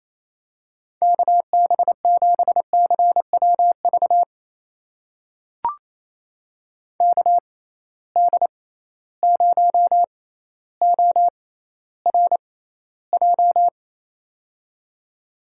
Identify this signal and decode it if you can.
Morse code, heard as a keyed tone, about 28 wpm, spaced as for 14 wpm, 700 Hz. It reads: KB7CWV KD0ORJ
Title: USA Call Signs - 6 Characters - 6x Character Spacing - 28wpm